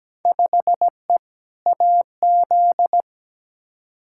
Morse code, heard as a keyed tone, about 17 wpm, 700 Hz. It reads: 5E AZ
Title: Field Day Exchanges - 17wpm